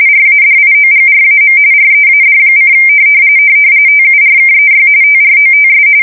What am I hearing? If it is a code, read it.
RTTY, about 45.45 baud. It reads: RADIOGRAM READ: TURKEY TROTS
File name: fsk170 / battleshipnj